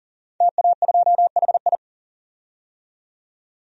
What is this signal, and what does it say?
Morse code, heard as a keyed tone, about 40 wpm, 700 Hz. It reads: TA2HI
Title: DXCC-100 Call Signs - 5 Characters - 40wpm